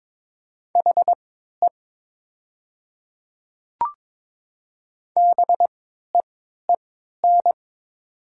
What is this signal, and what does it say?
Morse code, heard as a keyed tone, about 23 wpm, spaced as for 13 wpm, 700 Hz. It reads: HE BEEN
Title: 3x Character Spacing - Top 100 Words - 22wpm